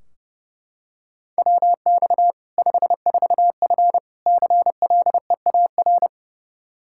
Morse code, heard as a keyed tone, about 30 wpm, 700 Hz. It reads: WX 54F CLEAR
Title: K1USN SST QSO Practice 30wpm